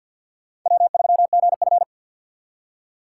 Morse code, heard as a keyed tone, about 51 wpm, 700 Hz. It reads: W3GF
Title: USA Call Signs - 4 Characters - 3x Character Spacing - 50wpm